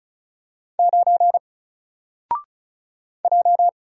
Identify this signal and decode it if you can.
Morse code, heard as a keyed tone, about 36 wpm, 700 Hz. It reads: 9 J
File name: Single-Letter-Number-35wpm